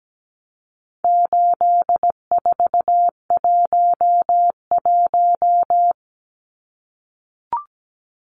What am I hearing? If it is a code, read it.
Morse code, heard as a keyed tone, about 17 wpm, 700 Hz. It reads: 8411
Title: Numbers - 4-Digit 17wpm